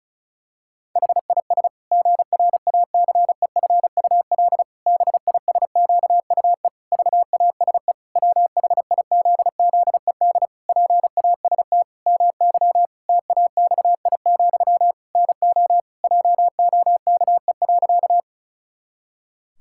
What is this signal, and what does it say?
Morse code, heard as a keyed tone, about 34 wpm, 700 Hz. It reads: HIS GRACEFUL BISQUE VASE WHIZZED PAST MY TAXI, NO JOKE.
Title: Pangram Challenge v2 - 35wpm